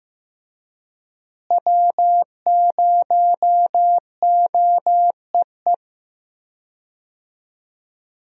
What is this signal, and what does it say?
Morse code, heard as a keyed tone, about 16 wpm, 700 Hz. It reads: W0OEE
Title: USA Call Signs - 5 Characters - Repeated 3x - 15wpm